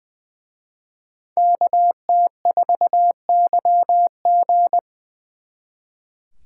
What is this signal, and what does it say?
Morse code, heard as a keyed tone, about 24 wpm, 700 Hz. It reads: KT4YG
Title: Speed Racing - USA 2x2 Call Signs 20wpm